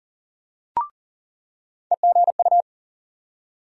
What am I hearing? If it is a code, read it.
Morse code, heard as a keyed tone, about 37 wpm, 700 Hz. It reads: EGU
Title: USA Call Sign Postfixes - 3 Letters 40wpm